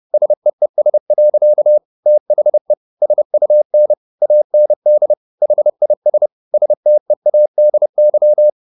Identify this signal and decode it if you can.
Morse code, heard as a keyed tone, about 30 wpm, 600 Hz. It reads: SEES. THE SUN AND HIS STEADY
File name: the old man and the sea / the old man and the sea 1 30wpm